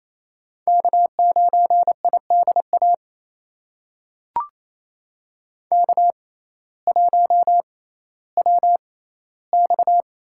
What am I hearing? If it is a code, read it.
Morse code, heard as a keyed tone, about 28 wpm, spaced as for 15 wpm, 700 Hz. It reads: K9IDA K1WX
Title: USA Call Signs - 5 Characters - 6x Character Spacing - 28wpm